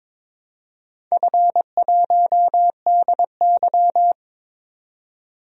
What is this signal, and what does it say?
Morse code, heard as a keyed tone, about 22 wpm, 700 Hz. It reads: F1DY
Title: DXCC-100 Call Signs - 4 Characters - 22wpm